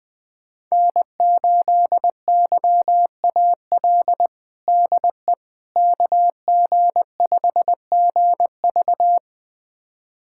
Morse code, heard as a keyed tone, about 20 wpm, 700 Hz. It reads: N8YAL DE KG5GV
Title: USA Callsign DE Callsign - 5 Characters - 20wpm